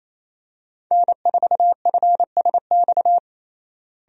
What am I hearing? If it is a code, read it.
Morse code, heard as a keyed tone, about 28 wpm, 700 Hz. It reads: N4FSX
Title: USA Call Signs - 5 Characters - Repeated 3x - 28wpm